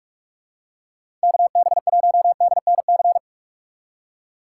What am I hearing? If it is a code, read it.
Morse code, heard as a keyed tone, about 46 wpm, 700 Hz. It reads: KB1DNC
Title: USA Call Signs - 6 Characters - 2x Character Spacing - 45wpm